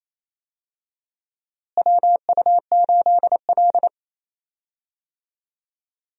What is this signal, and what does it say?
Morse code, heard as a keyed tone, about 28 wpm, 700 Hz. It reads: WU8L
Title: USA Call Signs - 4 Characters - Repeated 3x - 28wpm